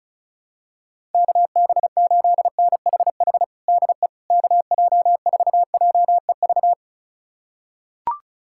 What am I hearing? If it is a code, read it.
Morse code, heard as a keyed tone, about 36 wpm, 700 Hz. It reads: KB8NHH DE KJ4JEV
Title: USA Callsign DE Callsign 35wpm